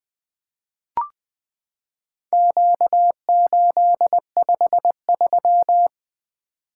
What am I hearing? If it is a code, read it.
Morse code, heard as a keyed tone, about 20 wpm, 700 Hz. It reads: Q853
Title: Four Letter-Number Combinations - 20wpm